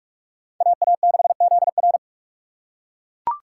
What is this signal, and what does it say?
Morse code, heard as a keyed tone, about 41 wpm, 700 Hz. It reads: AA6ZR